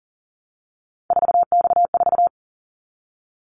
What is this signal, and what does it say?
Morse code, heard as a keyed tone, about 40 wpm, 700 Hz. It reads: 4X4
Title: Callsign Prefixes for Top 50 DXCC - 40wpm